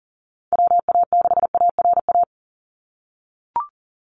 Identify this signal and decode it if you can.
Morse code, heard as a keyed tone, about 42 wpm, 700 Hz. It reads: WA6ARA
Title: NA SOTA Chaser Callsigns 40wpm